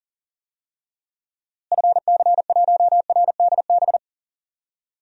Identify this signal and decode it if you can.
Morse code, heard as a keyed tone, about 41 wpm, 700 Hz. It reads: FC1RDB
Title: DXCC-100 Call Signs - 6 Characters - Repeated 3x - 40wpm